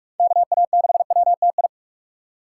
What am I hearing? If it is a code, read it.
Morse code, heard as a keyed tone, about 46 wpm, 700 Hz. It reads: KA6WTI
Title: USA Call Signs - 6 Characters - 6x Character Spacing - 45wpm